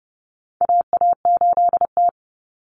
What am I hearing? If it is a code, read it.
Morse code, heard as a keyed tone, about 30 wpm, 700 Hz. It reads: AA8T
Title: Common Cut Numbers - 4-Digit 30wpm